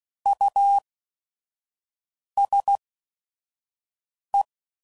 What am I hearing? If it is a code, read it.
Morse code, heard as a keyed tone, about 16 wpm, spaced as for 5 wpm, 800 Hz. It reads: USE
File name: Prefixes / By First Letter / Us